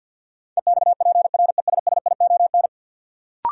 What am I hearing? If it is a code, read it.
Morse code, heard as a keyed tone, about 53 wpm, 700 Hz. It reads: EXPRESSION
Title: Sentences from Top 1200 Words - New Word Drill Alt - 50wpm